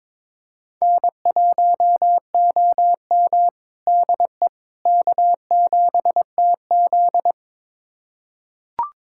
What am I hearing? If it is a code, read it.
Morse code, heard as a keyed tone, about 22 wpm, 700 Hz. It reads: N1OM DE K7TZ
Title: USA Callsign DE Callsign - 4 Characters - 22wpm